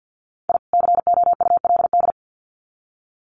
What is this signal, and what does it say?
Morse code, heard as a keyed tone, about 50 wpm, 700 Hz. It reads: I COULD
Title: Sets of 2 Words 50wpm